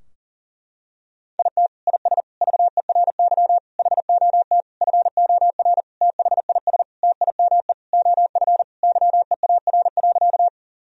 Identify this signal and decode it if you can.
Morse code, heard as a keyed tone, about 40 wpm, 700 Hz. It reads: IT IS VERY HOT FOR THIS TIME OF YEAR.
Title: Sentences from Top 1000 Words - Review Part 3 - 40wpm